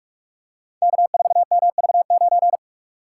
Morse code, heard as a keyed tone, about 42 wpm, 700 Hz. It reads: K4MV9
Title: Code Groups - 5 Letter:Numbers - 45wpm